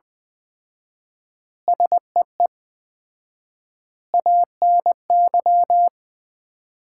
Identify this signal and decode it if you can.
Morse code, heard as a keyed tone, about 20 wpm, 700 Hz. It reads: SEE ANY
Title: 4x Word-Spacing - Sets of 5 Words - 20wpm